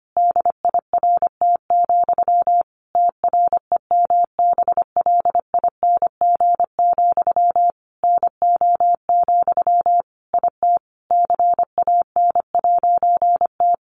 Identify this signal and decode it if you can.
Morse code, heard as a keyed tone, about 25 wpm, 700 Hz. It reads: DIRT, TREMBLING, NO, IT CAN'T
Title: A Sound Of Thunder 25wpm